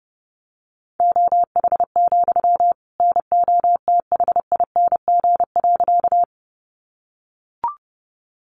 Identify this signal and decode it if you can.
Morse code, heard as a keyed tone, about 28 wpm, 700 Hz. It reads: OH, NOTHING.
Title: Sentences from Top 500 Words 30wpm